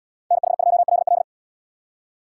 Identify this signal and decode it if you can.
Morse code, heard as a keyed tone, about 77 wpm, 700 Hz. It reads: NH2LL